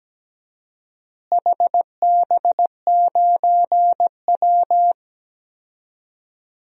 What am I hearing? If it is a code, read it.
Morse code, heard as a keyed tone, about 17 wpm, 700 Hz. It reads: HB9W
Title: DXCC-100 Call Signs - 4 Characters - Repeated 2x - 17wpm